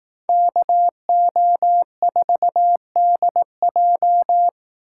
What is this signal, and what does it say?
Morse code, heard as a keyed tone, about 18 wpm, 700 Hz. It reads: KO4DJ
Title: Speed Racing - USA 2x2 Call Signs 15wpm